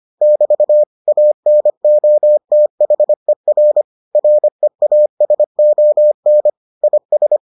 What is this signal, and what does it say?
Morse code, heard as a keyed tone, about 25 wpm, 600 Hz. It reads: <BT> ANOTHER REASON IS